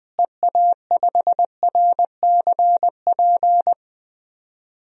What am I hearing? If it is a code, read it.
Morse code, heard as a keyed tone, about 20 wpm, 700 Hz. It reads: EA5RCP
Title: DXCC-100 Call Signs - 6 Characters - Repeated 3x - 20wpm